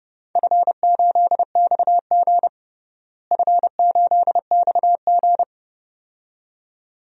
Morse code, heard as a keyed tone, about 29 wpm, 700 Hz. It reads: F8XG F8XG
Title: DXCC-100 Call Signs - 4 Characters - Repeated 2x - 30wpm